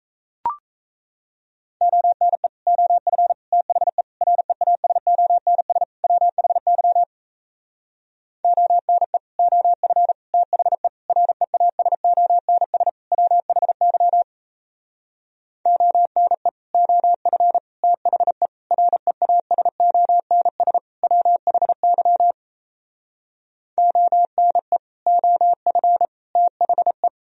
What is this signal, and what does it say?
Morse code, heard as a keyed tone, about 37 wpm, 700 Hz. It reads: ONE OF THE REASONS WHY ONE OF THE REASONS WHY ONE OF THE REASONS WHY ONE OF THE